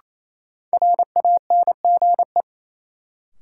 Morse code, heard as a keyed tone, about 28 wpm, 700 Hz. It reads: RANGE
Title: Sets of 4 Words - New Word Drill - 28wpm